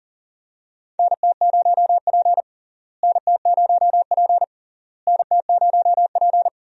Morse code, heard as a keyed tone, about 40 wpm, 700 Hz. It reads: NT0P NT0P NT0P
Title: USA Call Signs - 4 Characters - Repeated 3x - 40wpm